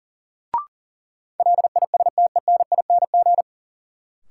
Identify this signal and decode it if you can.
Morse code, heard as a keyed tone, about 40 wpm, 700 Hz. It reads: LISTENING
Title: Sets of 5 Words - New Word Drill - 40wpm